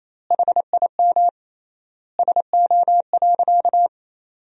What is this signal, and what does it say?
Morse code, heard as a keyed tone, about 28 wpm, 700 Hz. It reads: HIM SO.
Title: 3x Word-Spacing - Sentences from Top 200 Words - 28wpm